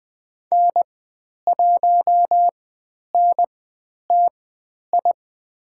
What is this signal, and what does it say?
Morse code, heard as a keyed tone, about 20 wpm, spaced as for 10 wpm, 700 Hz. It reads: N1NTI
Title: 102 Lesson 17g - Letter u - Call Signs 20wpm